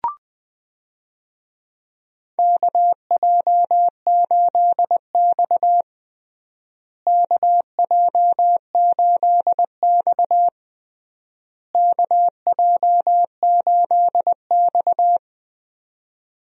Morse code, heard as a keyed tone, about 20 wpm, 700 Hz. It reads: KJ8X KJ8X KJ8X
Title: USA Call Signs - 4 Characters - Repeated 3x - 20wpm